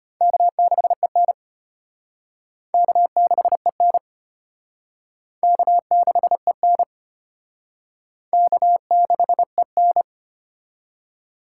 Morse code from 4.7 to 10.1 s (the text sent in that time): K6EN K6EN